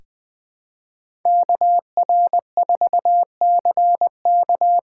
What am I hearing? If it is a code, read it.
Morse code, heard as a keyed tone, about 20 wpm, 700 Hz. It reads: KR4CK
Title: USA Call Signs - 6 Characters - 6x Character Spacing - 20wpm